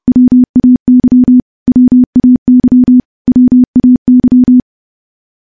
Morse code, heard as a keyed tone, about 30 wpm, 250 Hz. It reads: WAY WAY WAY
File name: morse / morse-training-30WPM